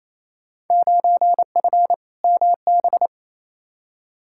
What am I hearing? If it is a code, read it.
Morse code, heard as a keyed tone, about 28 wpm, 700 Hz. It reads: 9F MB
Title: Field Day Exchanges Encore - 28wpm